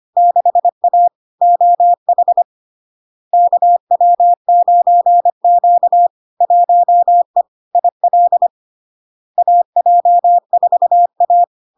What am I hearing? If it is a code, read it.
Morse code, heard as a keyed tone, about 25 wpm, 700 Hz. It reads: 6A OH KW9Q 1E IL AJ4A
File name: LARK-FD-2022-Log-in-CW-25wpm-700-Hz